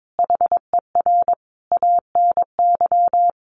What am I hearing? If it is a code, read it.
Morse code, heard as a keyed tone, about 22 wpm, 700 Hz. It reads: HER ANY
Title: Sentences from Top 1500 Words 22wpm